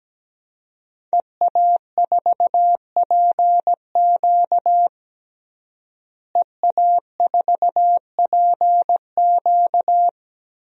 Morse code, heard as a keyed tone, about 17 wpm, 700 Hz. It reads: EA4PQ EA4PQ
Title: DXCC-100 Call Signs - 5 Characters - Repeated 2x - 17wpm